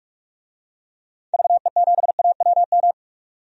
Morse code, heard as a keyed tone, about 46 wpm, 700 Hz. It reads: VE7AWM
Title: DXCC-100 Call Signs - 6 Characters - 45wpm